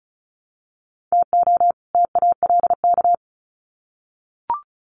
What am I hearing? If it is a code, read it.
Morse code, heard as a keyed tone, about 34 wpm, 700 Hz. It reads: TO TALK
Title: Sets of 2 Words 35wpm